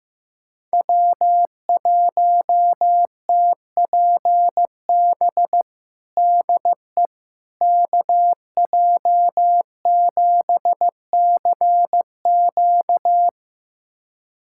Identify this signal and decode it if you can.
Morse code, heard as a keyed tone, about 15 wpm, 700 Hz. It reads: W1TPB DE KJ7CQ
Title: USA Callsign DE Callsign - 5 Characters - 15wpm